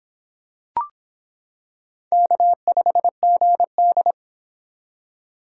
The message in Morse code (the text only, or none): K5GD